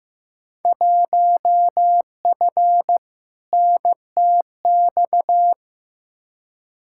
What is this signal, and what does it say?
Morse code, heard as a keyed tone, about 15 wpm, 700 Hz. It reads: 1F NTX
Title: Field Day Exchanges Encore - 15wpm